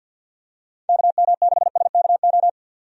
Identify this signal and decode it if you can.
Morse code, heard as a keyed tone, about 48 wpm, 700 Hz. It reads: KM6SKO
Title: USA Call Signs - 6 Characters - 50wpm